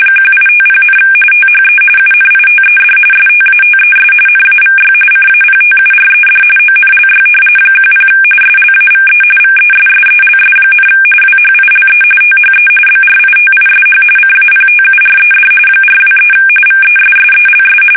RTTY, about 45.45 baud. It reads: 1850, THE UNITED STATES NAVY CAPTURED ONLY 10 SLAVE VESSELS, WHILE THE BRITISH CAPTURED 423 VESSELS
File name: fsk850 / usnavy2